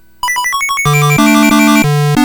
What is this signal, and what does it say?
There is no code.